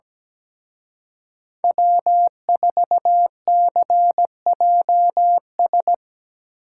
Morse code, heard as a keyed tone, about 19 wpm, 700 Hz. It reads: W4CJS
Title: Speed Racing - USA 1x3 Call Signs 17wpm